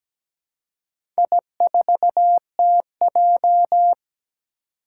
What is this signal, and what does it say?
Morse code, heard as a keyed tone, about 17 wpm, 700 Hz. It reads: I4TJ
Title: Four Challenging Letter-Number Combinations 17wpm